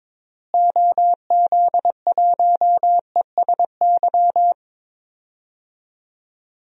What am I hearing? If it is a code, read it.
Morse code, heard as a keyed tone, about 22 wpm, 700 Hz. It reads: OZ1ESY
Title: DXCC-100 Call Signs - 6 Characters - Repeated 2x - 22wpm